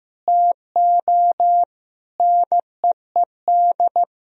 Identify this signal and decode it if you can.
Morse code, heard as a keyed tone, about 15 wpm, 700 Hz. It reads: TO NEED
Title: Sentences from Top 1000 Words - Review Part 1 - 15wpm